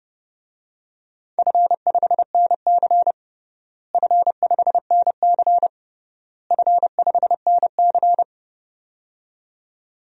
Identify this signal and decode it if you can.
Morse code, heard as a keyed tone, about 30 wpm, 700 Hz. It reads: F5NC F5NC F5NC
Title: DXCC-100 Call Signs - 4 Characters - Repeated 3x - 30wpm